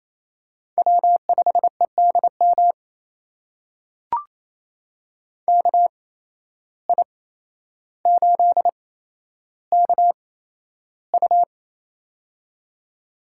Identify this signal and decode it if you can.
Morse code, heard as a keyed tone, about 28 wpm, spaced as for 12 wpm, 700 Hz. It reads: W5EDM KI8KU
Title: USA Call Signs - 5 Characters - 8x Character Spacing - 28wpm